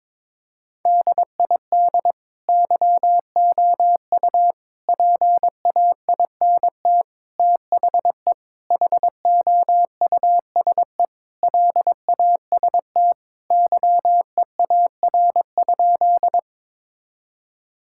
Morse code, heard as a keyed tone, about 22 wpm, 700 Hz. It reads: DID YOU PAINT THE HOUSE LAST YEAR?
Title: Sentences from Top 1000 Words - Review Part 6 - 22wpm